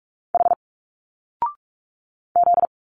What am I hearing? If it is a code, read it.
Morse code, heard as a keyed tone, about 43 wpm, 700 Hz. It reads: H Z
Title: Single Letters 45wpm